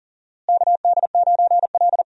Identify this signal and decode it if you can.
Morse code, heard as a keyed tone, about 40 wpm, 700 Hz. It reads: KD9L
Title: USA Call Signs - 4 Characters - Repeated 3x - 40wpm